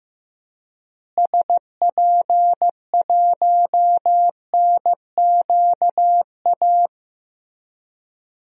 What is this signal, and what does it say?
Morse code, heard as a keyed tone, about 15 wpm, 700 Hz. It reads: SP1NQA
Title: DXCC-100 Call Signs - 6 Characters - Repeated 3x - 15wpm